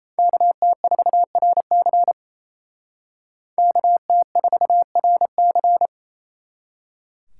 Morse code, heard as a keyed tone, about 36 wpm, 700 Hz. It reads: KT4RC KT4RC